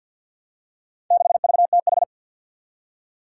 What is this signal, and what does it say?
Morse code, heard as a keyed tone, about 51 wpm, 700 Hz. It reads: 6VTH
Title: Four Challenging Letter-Number Combinations 50wpm